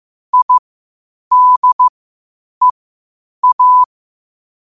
Morse code, heard as a keyed tone, about 15 wpm, spaced as for 8 wpm, 1000 Hz. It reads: IDEA
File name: qotdfiles / 8wpm